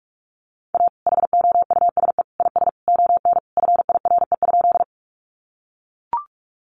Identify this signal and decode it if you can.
Morse code, heard as a keyed tone, about 44 wpm, 700 Hz. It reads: A HOUSE IS ON FIRE?
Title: Sentences from Top 400 Words 45wpm